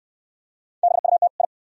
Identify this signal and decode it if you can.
Morse code, heard as a keyed tone, about 69 wpm, 690 Hz. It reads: BUT I